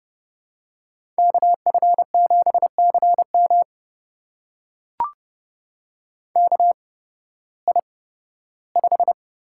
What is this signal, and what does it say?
Morse code, heard as a keyed tone, about 30 wpm, spaced as for 16 wpm, 700 Hz. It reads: KF7CM KI5
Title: USA Call Signs - 8x Character Spacing - 30wpm